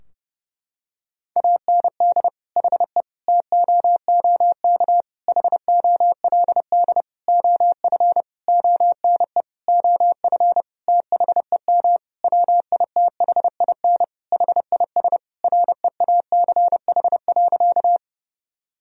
Morse code, heard as a keyed tone, about 32 wpm, 700 Hz. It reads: AND HE TOOK HOLD OF ONE OF THEM WITHIN HIS REACH.
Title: Sentences from Top 1000 Words - Review Part 4 - 30wpm